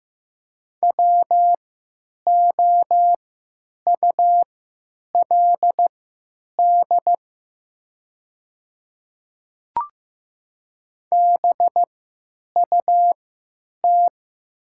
Morse code, heard as a keyed tone, about 15 wpm, spaced as for 9 wpm, 700 Hz. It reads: WOULD BUT
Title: 3x Character Spacing - Top 200 Words - 15wpm